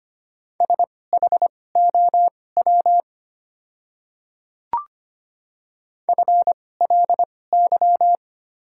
Morse code, heard as a keyed tone, about 25 wpm, spaced as for 18 wpm, 690 Hz. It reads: SHOW FLY
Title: 2x Character Spacing - Top 500 Words - 25wpm